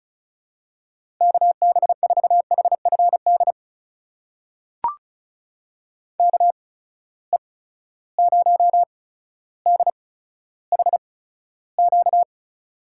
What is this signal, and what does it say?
Morse code, heard as a keyed tone, about 36 wpm, spaced as for 15 wpm, 700 Hz. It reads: KB4HFD KE0DHQ
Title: USA Call Signs - 6 Characters - 8x Character Spacing - 35wpm